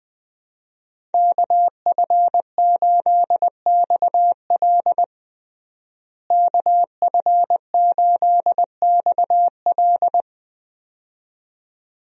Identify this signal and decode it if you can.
Morse code, heard as a keyed tone, about 20 wpm, 700 Hz. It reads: KF8XL KF8XL
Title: USA Call Signs - 5 Characters - Repeated 2x - 20wpm